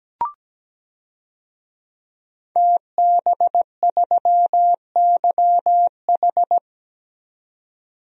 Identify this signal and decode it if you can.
Morse code, heard as a keyed tone, about 17 wpm, 700 Hz. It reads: TB3YH
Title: DXCC-100 Call Signs - 5 Characters - 17wpm